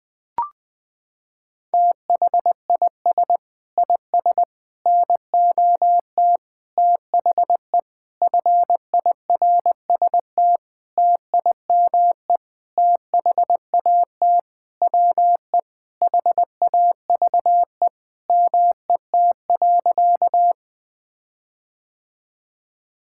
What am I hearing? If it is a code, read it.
Morse code, heard as a keyed tone, about 20 wpm, 700 Hz. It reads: THIS IS NOT THE FIRST TIME THAT WE HAVE MET.
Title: Sentences with 7-Word Phrases - 20wpm